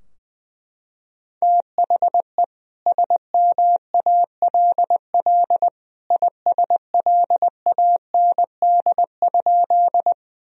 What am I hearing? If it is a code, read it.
Morse code, heard as a keyed tone, about 21 wpm, 700 Hz. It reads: THE SMALL ISLAND?
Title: Sentences from Top 1000 Words - Review Part 3 - 20wpm